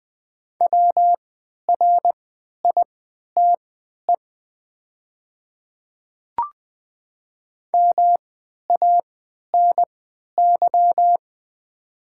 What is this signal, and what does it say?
Morse code, heard as a keyed tone, about 20 wpm, spaced as for 11 wpm, 700 Hz. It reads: WRITE MANY
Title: 3x Character Spacing - Top 100 Words - 20wpm